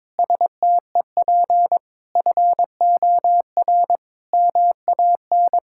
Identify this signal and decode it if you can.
Morse code, heard as a keyed tone, about 21 wpm, 700 Hz. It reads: STEP FOR MAN
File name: Sentences with 9-Word Phrases - 22wpm